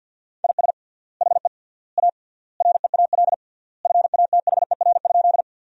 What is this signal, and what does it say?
Morse code, heard as a keyed tone, about 47 wpm, 700 Hz. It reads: IS HE A REAL FATHER?
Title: 3x Word-Spacing - Sentences from Top 300 Words - 50wpm